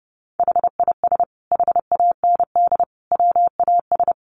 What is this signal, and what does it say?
Morse code, heard as a keyed tone, about 30 wpm, 700 Hz. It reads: HIS HAND WAS
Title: Sentences from Top 1100 Words 30wpm